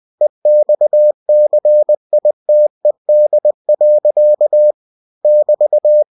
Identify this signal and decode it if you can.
Morse code, heard as a keyed tone, about 20 wpm, 600 Hz. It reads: EXCITED. <BT>
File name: cw-books / winnie-the-pooh / winnie-20wpm-page-0057